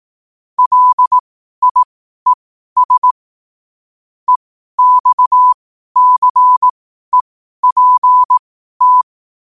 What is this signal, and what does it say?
Morse code, heard as a keyed tone, about 18 wpm, spaced as for 12 wpm, 1000 Hz. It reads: LIES EXCEPT